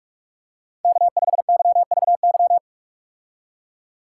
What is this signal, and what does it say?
Morse code, heard as a keyed tone, about 46 wpm, 700 Hz. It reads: K5YVY
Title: K1USN SST QSO Practice - 45wpm